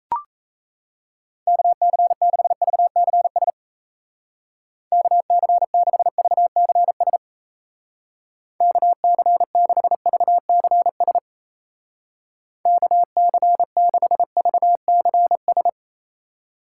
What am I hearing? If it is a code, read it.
Morse code, heard as a keyed tone, about 34 wpm, 700 Hz. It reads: KC6VCS KC6VCS KC6VCS KC6VCS
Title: Speed Racing - USA Call Signs 28wpm